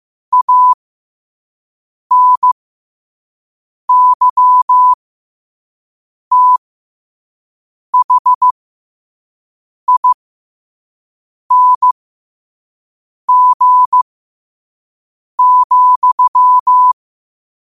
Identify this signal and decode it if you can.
Morse code, heard as a keyed tone, about 15 wpm, spaced as for 5 wpm, 1000 Hz. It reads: ANYTHING,